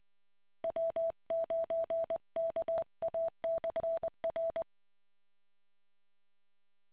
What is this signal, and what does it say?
Morse code, heard as a keyed tone, about 20 wpm, 660 Hz. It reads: W9KA/R